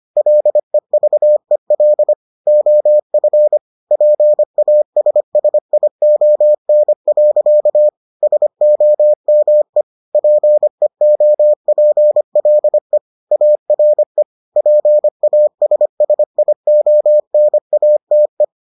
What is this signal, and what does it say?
Morse code, heard as a keyed tone, about 25 wpm, 600 Hz. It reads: LEVEL OF PASSION. SOME PEOPLE ARE PASSIONATE